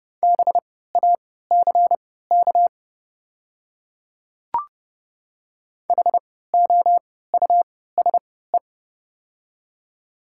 Morse code, heard as a keyed tone, about 30 wpm, spaced as for 17 wpm, 690 Hz. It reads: BACK HOUSE
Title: 3x Character Spacing - Top 200 Words - 30wpm